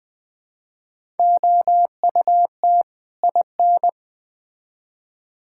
Morse code, heard as a keyed tone, about 21 wpm, 700 Hz. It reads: OUT IN
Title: Build A Phrase - 3-Word Phrases - 20wpm